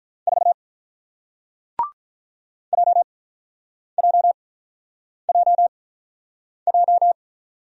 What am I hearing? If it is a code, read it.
Morse code, heard as a keyed tone, about 47 wpm, spaced as for 9 wpm, 700 Hz. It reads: 4 JJJJ